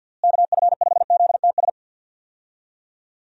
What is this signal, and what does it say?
Morse code, heard as a keyed tone, about 50 wpm, 700 Hz. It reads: KF5ZTS
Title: USA Call Signs - 6 Characters - Repeated 3x - 50wpm